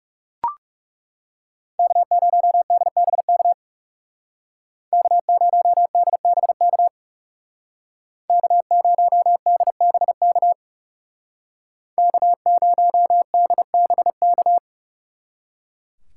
Morse code, heard as a keyed tone, about 37 wpm, 700 Hz. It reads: K0DBK K0DBK K0DBK K0DBK
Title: Speed Racing - USA 1x3 Call Signs 30wpm